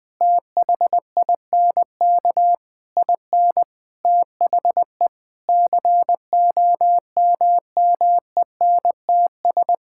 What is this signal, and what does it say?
Morse code, heard as a keyed tone, about 20 wpm, 700 Hz. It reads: THINK IN THE COMMENTS
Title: Sentences with 9-Word Phrases - 20wpm